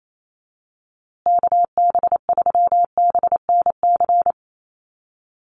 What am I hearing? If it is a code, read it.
Morse code, heard as a keyed tone, about 28 wpm, 700 Hz. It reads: KB3BNC
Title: USA Call Signs Repeated 3x 28wpm